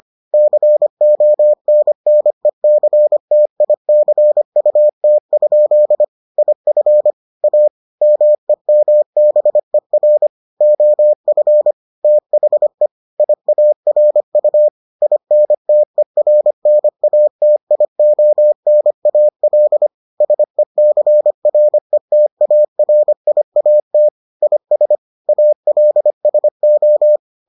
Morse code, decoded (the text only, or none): CONNECTICUT? IF A MEMBER OF THE IARU INTERNATIONAL SECRETARIAT IS ALSO